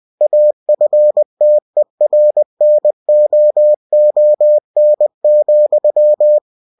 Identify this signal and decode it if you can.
Morse code, heard as a keyed tone, about 20 wpm, 600 Hz. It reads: AFTERNOON,
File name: cw-books / winnie-the-pooh / winnie-20wpm-page-0024